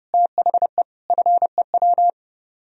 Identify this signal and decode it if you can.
Morse code, heard as a keyed tone, about 36 wpm, 700 Hz. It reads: THE FEW
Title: Speed-Racing - Sets of 5 Words 30wpm